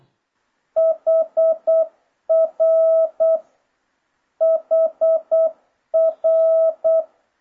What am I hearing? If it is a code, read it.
Morse code, heard as a keyed tone, about 8 wpm, 650 Hz. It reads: HR HR